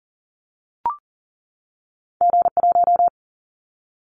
Morse code, heard as a keyed tone, about 40 wpm, 700 Hz. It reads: G1
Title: Callsign Prefixes for Top 25 DXCC - 40wpm